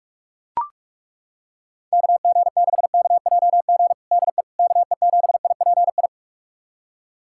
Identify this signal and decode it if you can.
Morse code, heard as a keyed tone, about 42 wpm, 700 Hz. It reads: KG6KJG DE KE7IPI
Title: USA Callsign DE Callsign - 6 Characters - 45wpm